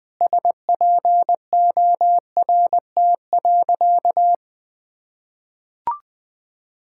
Morse code, heard as a keyed tone, about 20 wpm, 700 Hz. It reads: SPORT.
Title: Sentences from Top 1600 Words 20wpm